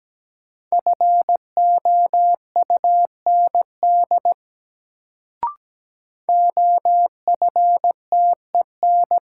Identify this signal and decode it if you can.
Morse code, heard as a keyed tone, about 17 wpm, 700 Hz. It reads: FOUND OFTEN
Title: Sentences from Top 300 Words - New Word Drill - 17wpm